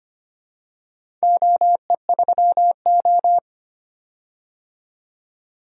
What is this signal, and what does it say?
Morse code, heard as a keyed tone, about 25 wpm, 700 Hz. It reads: OE3O
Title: DXCC-100 Call Signs - 4 Characters - 25wpm